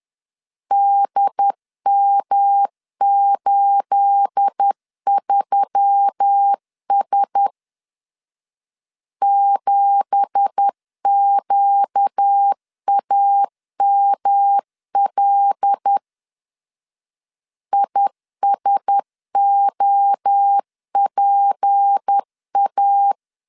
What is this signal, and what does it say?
Morse code, heard as a keyed tone, about 10 wpm, 800 Hz. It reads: DM83S 7QAML ISOPA
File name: CW / DET / 1 9 1500C S B 10wpm 4439